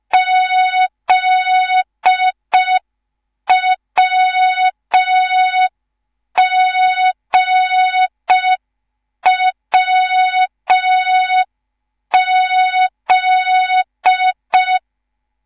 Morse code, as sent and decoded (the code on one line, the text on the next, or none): --.. .-- --. .-- --..
ZWGWZ